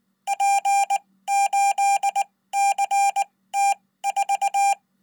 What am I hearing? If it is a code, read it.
Morse code, heard as a keyed tone, about 19 wpm, spaced as for 15 wpm, 800 Hz. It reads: P8CT4
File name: Code